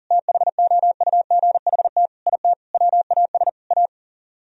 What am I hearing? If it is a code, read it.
Morse code, heard as a keyed tone, about 51 wpm, 700 Hz. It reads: THOUGHT IT WAS A